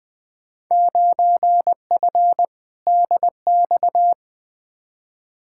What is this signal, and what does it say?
Morse code, heard as a keyed tone, about 20 wpm, 700 Hz. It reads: 9F DX
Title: Field Day Exchanges Encore - 20wpm